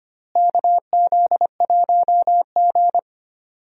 Morse code, heard as a keyed tone, about 25 wpm, 700 Hz. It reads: KZ1G
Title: USA Call Signs - 4 Characters - Repeated 3x - 25wpm